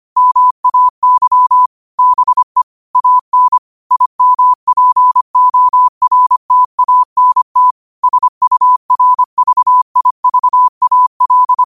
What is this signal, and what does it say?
Morse code, heard as a keyed tone, about 25 wpm, 1000 Hz. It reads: MAY BE AN IMPORTANT SURVIVAL
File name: qotdfiles / 25wpm